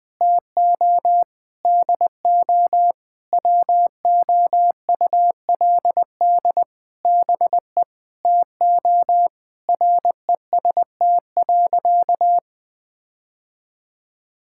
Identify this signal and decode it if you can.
Morse code, heard as a keyed tone, about 21 wpm, 700 Hz. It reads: TO DO WOULD BE TO REST.
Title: Sentences with 8-Word Phrases - 20wpm